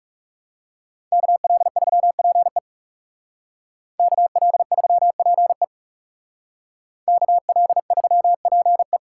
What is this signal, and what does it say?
Morse code, heard as a keyed tone, about 35 wpm, 700 Hz. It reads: KL3PE KL3PE KL3PE
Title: Speed Racing - USA 2x2 Call Signs 30wpm